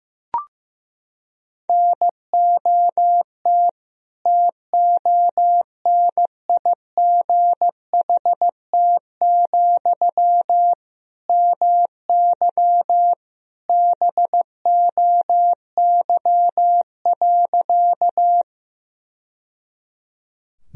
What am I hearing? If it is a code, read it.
Morse code, heard as a keyed tone, about 15 wpm, 700 Hz. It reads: NOT TONIGHT, MY BOY.
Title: Sentences from Top 1000 Words - Review Part 3 - 15wpm